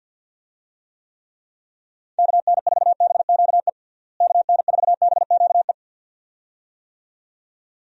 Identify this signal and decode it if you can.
Morse code, heard as a keyed tone, about 52 wpm, 700 Hz. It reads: KN4BQE KN4BQE
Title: USA Call Signs - 6 Characters - Repeated 2x - 50wpm